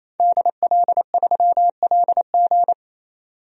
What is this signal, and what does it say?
Morse code, heard as a keyed tone, about 28 wpm, 700 Hz. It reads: DL3LG
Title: DXCC-100 Call Signs - 5 Characters - 28wpm